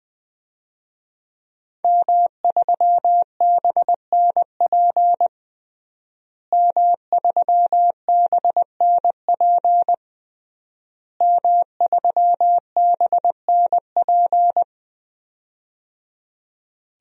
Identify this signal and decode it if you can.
Morse code, heard as a keyed tone, about 20 wpm, 700 Hz. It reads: M3BNP M3BNP M3BNP
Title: DXCC-100 Call Signs - 5 Characters - Repeated 3x - 20wpm